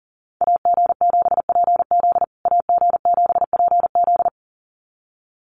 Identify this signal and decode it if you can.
Morse code, heard as a keyed tone, about 40 wpm, 700 Hz. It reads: AG7PZ AG7PZ
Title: USA Call Signs Repeated 2x 40wpm